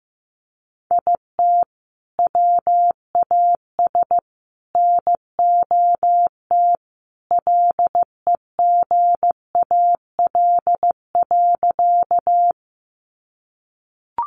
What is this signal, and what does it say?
Morse code, heard as a keyed tone, about 15 wpm, 700 Hz. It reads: IT WAS NOT LEGAL.
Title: Sentences from Top 1200 Words 15wpm